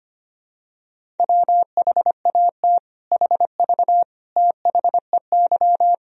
Morse code, heard as a keyed tone, about 25 wpm, 700 Hz. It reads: WHAT HV THEY
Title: Sentences from Top 500 Words and Abbreviations 25wpm